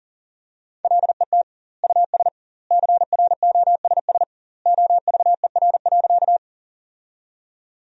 Morse code, heard as a keyed tone, about 40 wpm, 700 Hz. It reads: LET US CROSS OVER.
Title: 2x Word-Spacing - Sentences from Top 300 Words - 40wpm